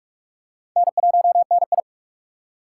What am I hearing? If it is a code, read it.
Morse code, heard as a keyed tone, about 45 wpm, 700 Hz. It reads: N1NI
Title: USA Call Signs - 4 Characters - 4x Character Spacing - 45wpm